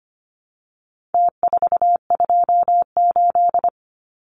Morse code, heard as a keyed tone, about 25 wpm, 700 Hz. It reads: T428
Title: Numbers - Contest Style - 4-Digit 25wpm